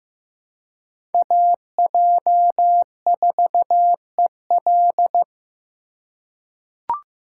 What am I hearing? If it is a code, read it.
Morse code, heard as a keyed tone, about 15 wpm, 700 Hz. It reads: AJ4EL
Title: USA Call Signs - 3x Character Spacing - 15wpm